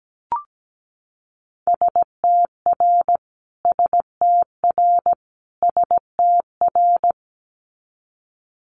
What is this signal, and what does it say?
Morse code, heard as a keyed tone, about 17 wpm, 700 Hz. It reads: STR STR STR
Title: Multi-letter Phonemes Repeated 3x 17wpm